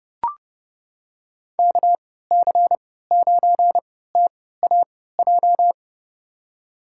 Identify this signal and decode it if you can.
Morse code, heard as a keyed tone, about 30 wpm, spaced as for 17 wpm, 700 Hz. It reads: KC9TAJ
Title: USA Call Signs - 3x Character Spacing - 30wpm